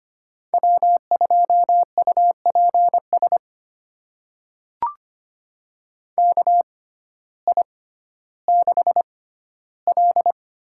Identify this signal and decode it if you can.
Morse code, heard as a keyed tone, about 25 wpm, spaced as for 14 wpm, 700 Hz. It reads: W2UPS KI6L
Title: USA Call Signs - 6x Character Spacing - 25wpm